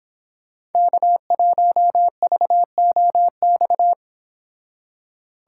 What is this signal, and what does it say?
Morse code, heard as a keyed tone, about 31 wpm, 700 Hz. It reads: K1VOX